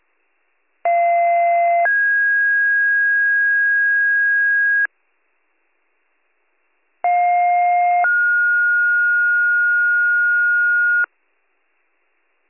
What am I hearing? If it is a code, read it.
Two-tone page, heard as a one-second tone, then a three-second tone, about 690 then 1650 Hz, about 710 then 1430 Hz. It is heard from 0.8 to 11.0 s.